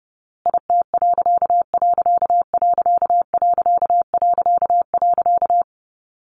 Morse code, heard as a keyed tone, about 30 wpm, 700 Hz. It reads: IT......
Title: A Sound Of Thunder 30wpm